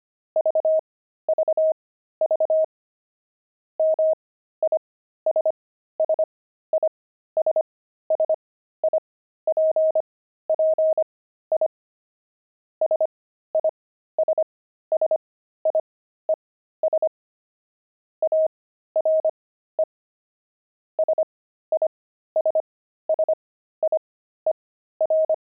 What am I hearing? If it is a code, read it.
Morse code, heard as a keyed tone, about 25 wpm, spaced as for 13 wpm, 630 Hz. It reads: VVV MISSISSIPPI SISSIES ARE SISSIER
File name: pangram-fs-13-cs-25-sissy